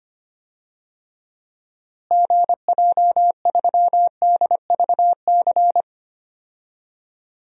Morse code, heard as a keyed tone, about 25 wpm, 700 Hz. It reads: GJ3DVC
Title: DXCC-100 Call Signs - 6 Characters - Repeated 2x - 25wpm